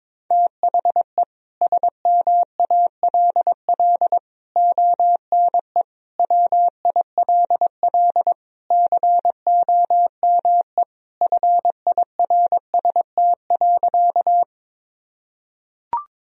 Morse code, heard as a keyed tone, about 22 wpm, 700 Hz. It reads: THE SMALL ONE WILL COME FIRST.
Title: Sentences from Top 1000 Words - Review Part 1 - 22wpm